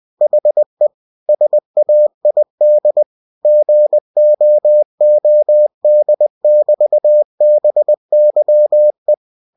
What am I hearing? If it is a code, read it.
Morse code, heard as a keyed tone, about 20 wpm, 600 Hz. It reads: HE SAID GOOD<BT>BYE
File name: cw-books / winnie-the-pooh / winnie-20wpm-page-0004